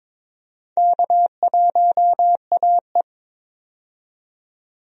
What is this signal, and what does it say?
Morse code, heard as a keyed tone, about 21 wpm, 700 Hz. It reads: K1AE
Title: USA Call Signs - 4 Characters - 22wpm